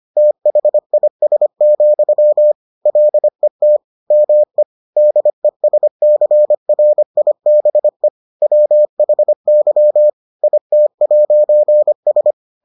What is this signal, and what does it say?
Morse code, heard as a keyed tone, about 25 wpm, 600 Hz. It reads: THIS, LET ME DESCRIBE WHY IT'S